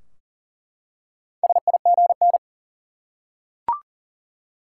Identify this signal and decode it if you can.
Morse code, heard as a keyed tone, about 40 wpm, 700 Hz. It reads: SIGN
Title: Four Letter Words - 40wpm